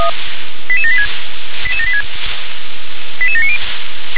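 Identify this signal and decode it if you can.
Five-tone selective call (ZVEI), heard as quick runs of five tones, tones of about 70 ms. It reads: 88786 88786 88790
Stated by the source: Five-tone ZVEI selective call received by SDR.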